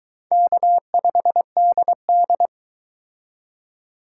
Morse code, heard as a keyed tone, about 22 wpm, 700 Hz. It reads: K5DD